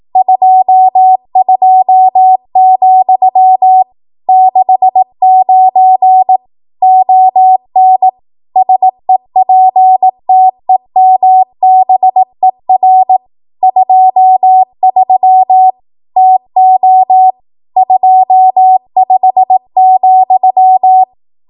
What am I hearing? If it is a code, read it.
Morse code, heard as a keyed tone, about 18 wpm, 750 Hz. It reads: 22, 69 ON SEPTEMBER 23 TO 25,
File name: Archive / 18 WPM / 18WPM